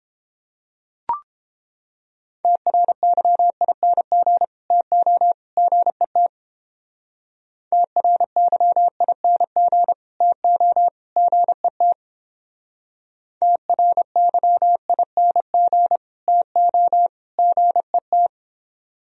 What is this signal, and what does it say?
Morse code, heard as a keyed tone, about 31 wpm, 700 Hz. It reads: TRYING TO GET TRYING TO GET TRYING TO GET